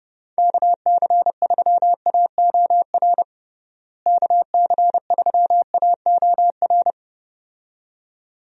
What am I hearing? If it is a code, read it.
Morse code, heard as a keyed tone, about 32 wpm, 700 Hz. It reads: KC3AOR KC3AOR
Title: USA Call Signs - 6 Characters - Repeated 2x - 30wpm